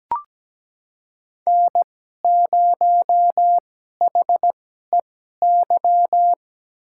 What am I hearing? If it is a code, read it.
Morse code, heard as a keyed tone, about 17 wpm, spaced as for 12 wpm, 700 Hz. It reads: N0HEY
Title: USA Call Signs - 5 Characters - 2x Character Spacing - 17wpm